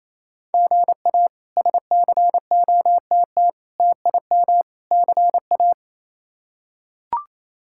Morse code, heard as a keyed tone, about 28 wpm, 700 Hz. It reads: GA SCOTT TIM CA
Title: K1USN SST QSO Practice - 28wpm